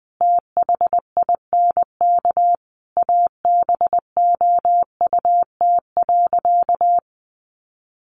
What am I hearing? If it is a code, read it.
Morse code, heard as a keyed tone, about 20 wpm, 700 Hz. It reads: THINK ABOUT.
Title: Sentences from Top 1100 Words 20wpm